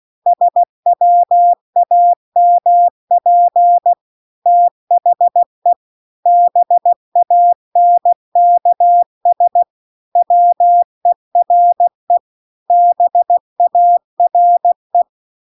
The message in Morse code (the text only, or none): SWAMP THE BANKS WERE BARE